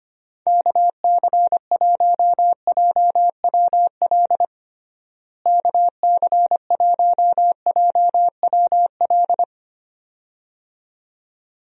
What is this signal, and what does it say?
Morse code, heard as a keyed tone, about 25 wpm, 700 Hz. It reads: KC1JWL KC1JWL
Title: USA Call Signs - 6 Characters - Repeated 2x - 25wpm